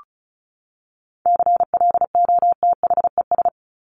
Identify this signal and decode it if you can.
Morse code, heard as a keyed tone, about 34 wpm, 700 Hz. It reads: CLOTHES
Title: Top 1751-2000 Words 35wpm